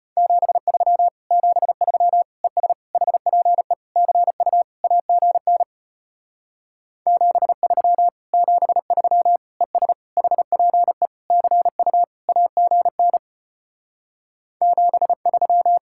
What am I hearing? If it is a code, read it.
Morse code, heard as a keyed tone, about 33 wpm, 700 Hz. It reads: 73 73 ES HPE CU AGN 73 73 ES HPE CU AGN 73